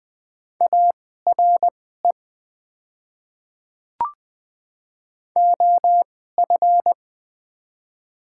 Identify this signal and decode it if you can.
Morse code, heard as a keyed tone, about 20 wpm, spaced as for 15 wpm, 690 Hz. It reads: ARE OF
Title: 2x Character Spacing - Top 100 Words - 20wpm